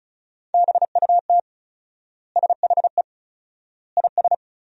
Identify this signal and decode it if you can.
Morse code, heard as a keyed tone, about 34 wpm, 700 Hz. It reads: BUT SHE IS
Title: 4x Word-Spacing - Sentences from Top 500 Words - 35wpm